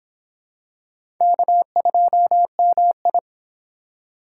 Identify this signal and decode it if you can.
Morse code, heard as a keyed tone, about 24 wpm, 700 Hz. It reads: K2MI